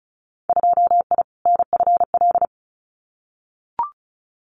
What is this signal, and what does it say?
Morse code, heard as a keyed tone, about 36 wpm, 700 Hz. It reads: 2I NFL
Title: Winter Field Day Exchanges 35wpm